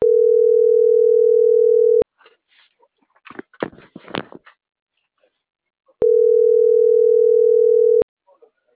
Ringback tone (North America), 440 + 480 Hz.